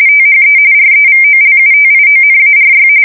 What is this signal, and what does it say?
RTTY, about 45.45 baud. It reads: THE REVOLUTIONARY
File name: fsk170 / usnavy1